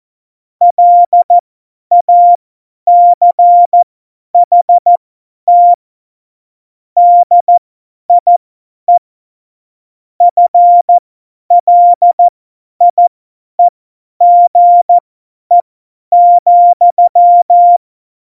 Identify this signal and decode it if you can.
Morse code, heard as a keyed tone, about 14 wpm, spaced as for 10 wpm, 700 Hz. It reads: LACHT DIE FLIEGE,
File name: Morsekurs / Witze / Witz 24 April 2018, 50 F70